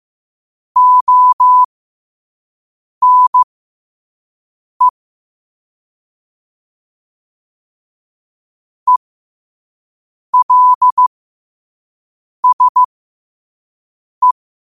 Morse code, heard as a keyed tone, about 15 wpm, spaced as for 5 wpm, 1000 Hz. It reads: ONE ELSE
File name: qotdfiles / 5wpm